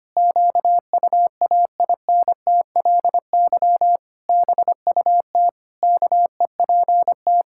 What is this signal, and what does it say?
Morse code, heard as a keyed tone, about 25 wpm, 700 Hz. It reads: QUAINTLY BUT KEPT
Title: Pangram Challenge v2 - 25wpm